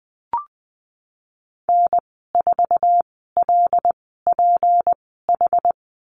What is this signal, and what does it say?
Morse code, heard as a keyed tone, about 20 wpm, spaced as for 15 wpm, 700 Hz. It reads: N4LPH
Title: 179 Lesson 29g - Slash - Call Signs 20wpm